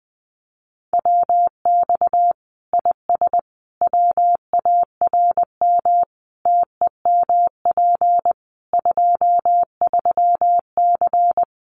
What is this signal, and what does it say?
Morse code, heard as a keyed tone, about 20 wpm, 700 Hz. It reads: WX IS WARM TEMP 23C
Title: Common QSO Phrases 20wpm